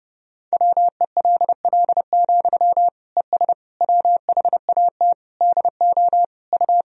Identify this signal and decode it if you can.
Morse code, heard as a keyed tone, about 30 wpm, 700 Hz. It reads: WELL, ES WHAT DO U
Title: Sentences from Top 200 Words and Abbreviations 30wpm